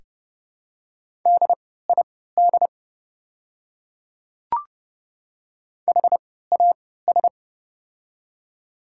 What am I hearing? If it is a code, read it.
Morse code, heard as a keyed tone, about 30 wpm, spaced as for 17 wpm, 690 Hz. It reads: DID HAS
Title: 3x Character Spacing - Top 100 Words - 30wpm